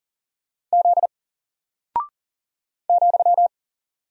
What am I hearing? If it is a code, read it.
Morse code, heard as a keyed tone, about 40 wpm, 700 Hz. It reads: Z ,
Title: Single Character (Letter:Number:Punctuation:Prosign) Alt 40wpm